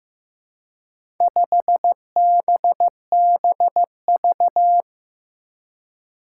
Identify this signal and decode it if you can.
Morse code, heard as a keyed tone, about 15 wpm, 700 Hz. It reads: 5BBV
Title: Four Challenging Letter-Number Combinations 15wpm